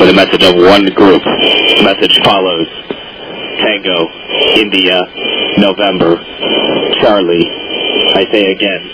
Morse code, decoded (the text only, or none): none